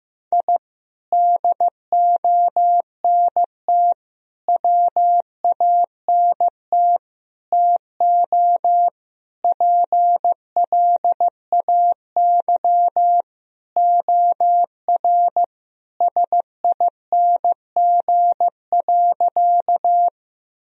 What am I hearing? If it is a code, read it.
Morse code, heard as a keyed tone, about 15 wpm, 700 Hz. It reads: I DONT WANT TO PLAY OR SING.
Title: Sentences from Top 1000 Words - Review Part 2 - 15wpm